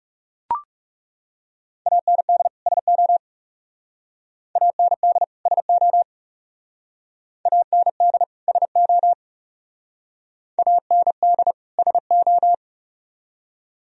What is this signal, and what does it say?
Morse code, heard as a keyed tone, about 37 wpm, 700 Hz. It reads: AND SO AND SO AND SO AND SO
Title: Speed-Racing - Sets of 2 Words 30wpm